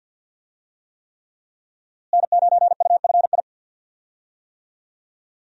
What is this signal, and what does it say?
Morse code, heard as a keyed tone, about 52 wpm, 700 Hz. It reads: N9UFI
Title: USA Call Signs - 5 Characters - Repeated 3x - 50wpm